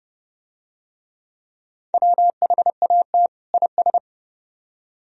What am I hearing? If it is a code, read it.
Morse code, heard as a keyed tone, about 26 wpm, 700 Hz. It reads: WHAT IS